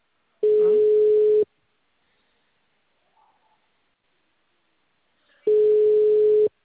Ringback tone (Europe, 425 Hz).